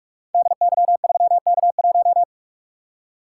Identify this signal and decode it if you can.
Morse code, heard as a keyed tone, about 44 wpm, 700 Hz. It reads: DY3K1
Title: Code Groups - 5 Letter:Numbers - 45wpm